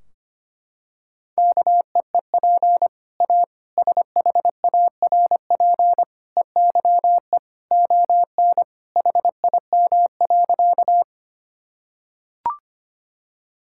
Morse code, heard as a keyed tone, about 25 wpm, 700 Hz. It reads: KEEP A SHARP EYE ON HIM.
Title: Sentences from Top 1600 Words 25wpm